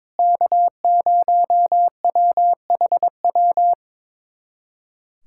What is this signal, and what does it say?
Morse code, heard as a keyed tone, about 22 wpm, 700 Hz. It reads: K0WHW
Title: USA Callsign DE Callsign - 5 Characters - 22wpm